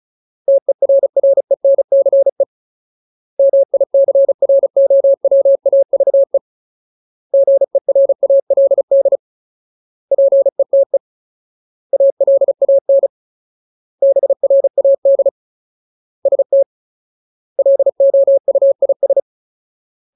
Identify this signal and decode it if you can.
Morse code, heard as a keyed tone, about 35 wpm, 550 Hz. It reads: TERRENCE MICROWAVE GERALD PETE ALAN BRAD ST LOUIS
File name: Ultimate Word List / 35WPM Ultimate